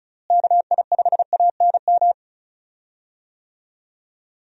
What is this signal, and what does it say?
Morse code, heard as a keyed tone, about 36 wpm, 700 Hz. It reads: KI5ANM
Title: USA Call Signs - 6 Characters - Repeated 3x - 35wpm